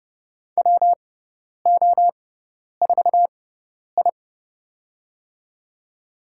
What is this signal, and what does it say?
Morse code, heard as a keyed tone, about 30 wpm, spaced as for 10 wpm, 700 Hz. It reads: WO4I
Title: USA Call Signs - 6x Character Spacing - 30wpm